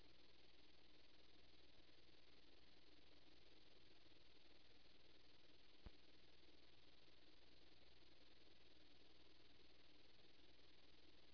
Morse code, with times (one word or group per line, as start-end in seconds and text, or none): none